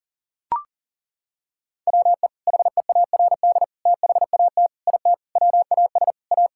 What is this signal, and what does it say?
Morse code, heard as a keyed tone, about 37 wpm, 700 Hz. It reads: WE HEARD THAT IT WAS A
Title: Sentences from Top 1000 Words - Review Part 6 - 40wpm